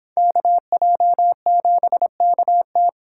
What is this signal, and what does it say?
Morse code, heard as a keyed tone, about 31 wpm, 700 Hz. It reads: KJ7KT